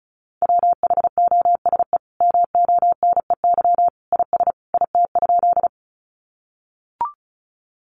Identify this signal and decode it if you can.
Morse code, heard as a keyed tone, about 34 wpm, 700 Hz. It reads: WHOSE MONEY IS IT?
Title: Sentences from Top 1200 Words 35wpm